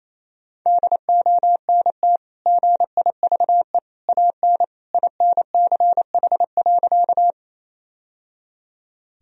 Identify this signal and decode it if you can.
Morse code, heard as a keyed tone, about 28 wpm, 700 Hz. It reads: DONT GIVE AN INCH.
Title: Sentences from Top 1000 Words - Review Part 3 - 28wpm